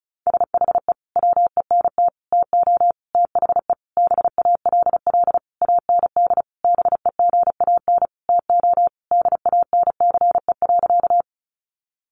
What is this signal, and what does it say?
Morse code, heard as a keyed tone, about 34 wpm, 700 Hz. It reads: SHE WENT TO THE BALL AND BEGAN TO DANCE.
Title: Sentences from Top 900 Words 35wpm